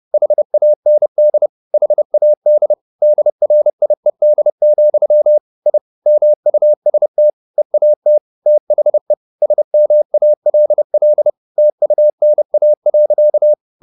Morse code, heard as a keyed tone, about 30 wpm, 600 Hz. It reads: HAND HAD DRIED, I MUST EAT THE SMALL TUNA.
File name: the old man and the sea / the old man and the sea 1 30wpm